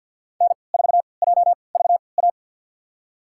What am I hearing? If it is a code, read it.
Morse code, heard as a keyed tone, about 47 wpm, spaced as for 28 wpm, 690 Hz. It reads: N4JVA
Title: USA Call Signs - 5 Characters - 3x Character Spacing - 50wpm